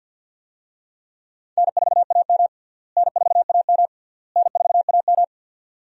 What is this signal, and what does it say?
Morse code, heard as a keyed tone, about 47 wpm, 700 Hz. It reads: N4AM N4AM N4AM
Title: USA Call Signs - 4 Characters - Repeated 3x - 50wpm